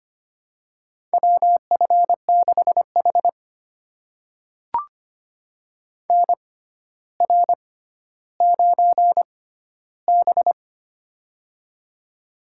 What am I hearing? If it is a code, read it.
Morse code, heard as a keyed tone, about 25 wpm, spaced as for 13 wpm, 700 Hz. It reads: WF6H NR9B
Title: USA Call Signs - 4 Characters - 6x Character Spacing - 25wpm